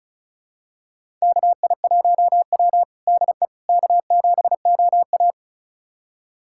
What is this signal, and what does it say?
Morse code, heard as a keyed tone, about 34 wpm, 690 Hz. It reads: KI1W DE K7OA
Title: USA Callsign DE Callsign - 4 Characters - 35wpm